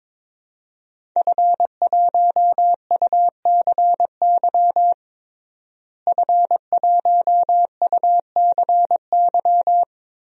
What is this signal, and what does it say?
Morse code, heard as a keyed tone, about 22 wpm, 700 Hz. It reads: F1UCY F1UCY
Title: DXCC-100 Call Signs - 5 Characters - Repeated 3x - 22wpm